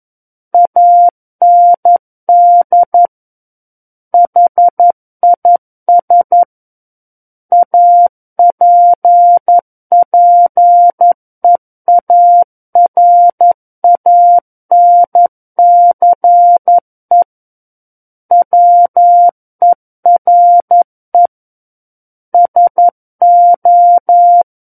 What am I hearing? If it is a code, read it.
Morse code, heard as a keyed tone, about 11 wpm, 700 Hz. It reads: AND HIS APPEARANCE WERE SO
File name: Book One / 02-Chapter 2 - 11 WPM